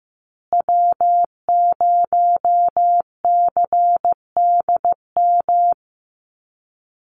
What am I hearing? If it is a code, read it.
Morse code, heard as a keyed tone, about 15 wpm, 700 Hz. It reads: W0CDM
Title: USA Call Signs Repeated 3x 15wpm